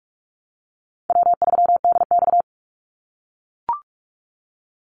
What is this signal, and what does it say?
Morse code, heard as a keyed tone, about 44 wpm, 700 Hz. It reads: W3DX
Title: NA SOTA Activator Callsigns 45wpm